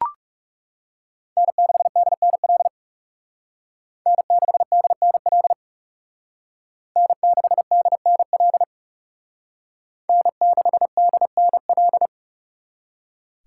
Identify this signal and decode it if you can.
Morse code, heard as a keyed tone, about 38 wpm, 700 Hz. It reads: N6DNL N6DNL N6DNL N6DNL